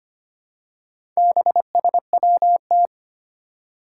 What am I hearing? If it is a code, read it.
Morse code, heard as a keyed tone, about 25 wpm, 700 Hz. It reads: BSWT
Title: Four Challenging Letter-Number Combinations 25wpm